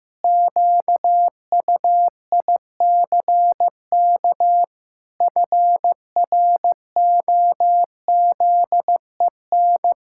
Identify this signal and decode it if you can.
Morse code, heard as a keyed tone, about 15 wpm, 700 Hz. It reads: QUICK FROZEN
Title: Pangram Challenge v2 - 15wpm